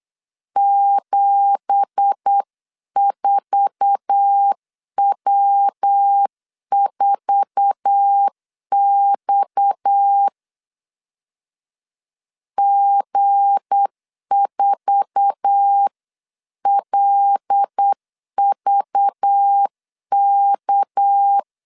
Morse code, heard as a keyed tone, about 8 wpm, 800 Hz. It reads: 74W4X G4LVK
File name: CW / DET / 1 9 1500C S A 8wpm 5637